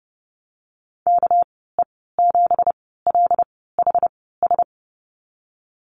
Morse code, heard as a keyed tone, about 30 wpm, spaced as for 17 wpm, 700 Hz. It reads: KE7LHS
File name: Lesson 37g - Letter x - Call Signs 30wpm